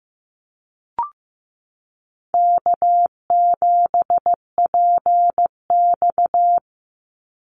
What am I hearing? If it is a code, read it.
Morse code, heard as a keyed tone, about 15 wpm, 700 Hz. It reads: K7PX
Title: NA SOTA Activator Callsigns 15wpm